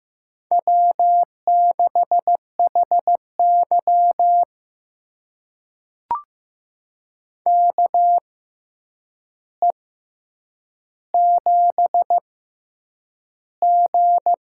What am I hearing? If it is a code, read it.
Morse code, heard as a keyed tone, about 15 wpm, spaced as for 8 wpm, 700 Hz. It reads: W6HY KE7G
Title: USA Call Signs - 4 Characters - 6x Character Spacing - 15wpm